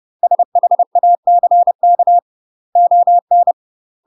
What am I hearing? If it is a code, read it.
Morse code, heard as a keyed tone, about 30 wpm, 700 Hz. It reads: SHACK ON